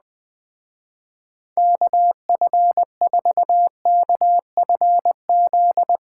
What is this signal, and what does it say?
Morse code, heard as a keyed tone, about 20 wpm, 700 Hz. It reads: KF4KFZ
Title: Speed Racing - USA 2x3 Call Signs 17wpm